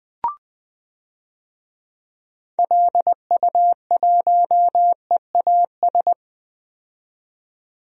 Morse code, heard as a keyed tone, about 20 wpm, 700 Hz. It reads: LU1EAS
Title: DXCC-100 Call Signs - 6 Characters - 20wpm